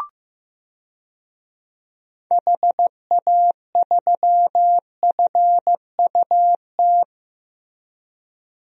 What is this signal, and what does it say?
Morse code, heard as a keyed tone, about 15 wpm, 700 Hz. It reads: HA3FUT
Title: DXCC-100 Call Signs - 6 Characters - Repeated 2x - 15wpm